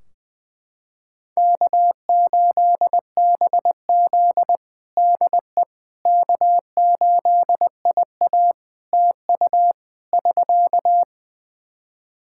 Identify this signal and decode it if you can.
Morse code, heard as a keyed tone, about 20 wpm, 700 Hz. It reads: K8BZ DE K8IA TU <SK>
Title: K1USN SST QSO Practice 20wpm